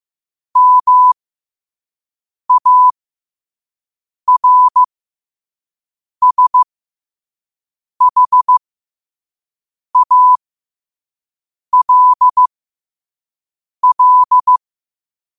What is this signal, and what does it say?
Morse code, heard as a keyed tone, about 15 wpm, spaced as for 5 wpm, 1000 Hz. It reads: MARSHALL
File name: qotdfiles / 5wpm